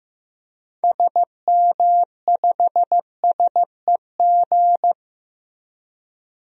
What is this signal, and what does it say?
Morse code, heard as a keyed tone, about 15 wpm, 700 Hz. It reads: SM5SEG
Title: DXCC-100 Call Signs - 6 Characters - Repeated 3x - 15wpm